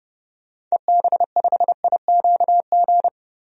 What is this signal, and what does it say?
Morse code, heard as a keyed tone, about 30 wpm, 700 Hz. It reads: EB5IQG
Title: DXCC-100 Call Signs - 6 Characters - 30wpm